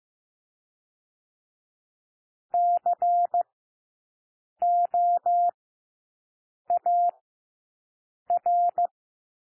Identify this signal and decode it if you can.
Morse code, heard as a keyed tone, about 15 wpm, 700 Hz. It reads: C O A R
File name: cwmp3 / W1 SOUND WORDS 15 wpm